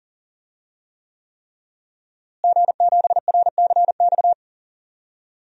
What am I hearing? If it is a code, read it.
Morse code, heard as a keyed tone, about 40 wpm, 700 Hz. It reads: G7RCX
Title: DXCC-100 Call Signs - 5 Characters - Repeated 3x - 40wpm